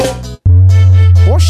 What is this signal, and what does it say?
Speech. There is no code.